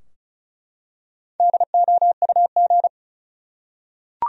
Morse code, heard as a keyed tone, about 36 wpm, 700 Hz. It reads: DOUG
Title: USA Common Names - Max 4 Letters - 35wpm